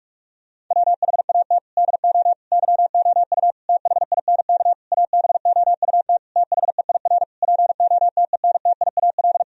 Morse code, heard as a keyed tone, about 46 wpm, 700 Hz. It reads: WHAT DO YOU THINK ABOUT THEIR POTENTIAL
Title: Sentences from Top 1000 Words - Review Part 4 - 45wpm